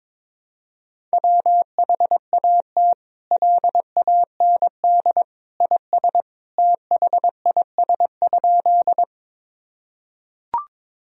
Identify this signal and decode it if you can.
Morse code, heard as a keyed tone, about 22 wpm, 700 Hz. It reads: WHAT LAND IS THIS?
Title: Sentences from Top 1000 Words - Review Part 1 - 22wpm